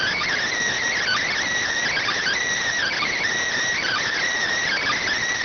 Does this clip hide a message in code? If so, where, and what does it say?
Yes: Emergency Alert System (SAME) tones from 1.0 to 5.1 s, heard as harsh buzzing bursts.